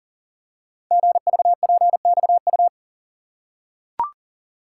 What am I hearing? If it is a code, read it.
Morse code, heard as a keyed tone, about 42 wpm, 690 Hz. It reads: GVPXU
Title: Code Groups - 5 Letters - 40wpm